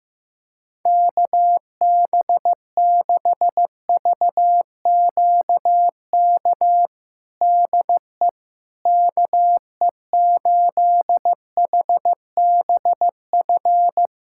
Morse code, heard as a keyed tone, about 15 wpm, 700 Hz. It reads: KB6VQK DE KE8HBF
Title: USA Callsign DE Callsign - 6 Characters - 15wpm